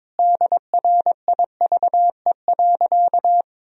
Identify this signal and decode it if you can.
Morse code, heard as a keyed tone, about 22 wpm, 700 Hz. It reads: DRIVE.
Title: Sentences from Top 1000 Words - Review Part 2 - 22wpm